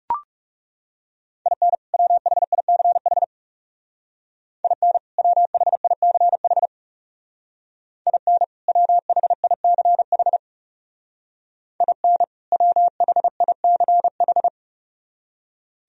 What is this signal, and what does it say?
Morse code, heard as a keyed tone, about 36 wpm, 700 Hz. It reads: IN WHICH IN WHICH IN WHICH IN WHICH
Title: Speed-Racing - Sets of 2 Words 30wpm